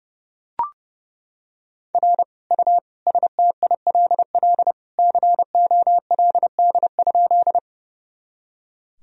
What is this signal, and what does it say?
Morse code, heard as a keyed tone, about 32 wpm, 700 Hz. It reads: R U STILL COLD?
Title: Sentences from Top 400 Words and Abbreviations 30wpm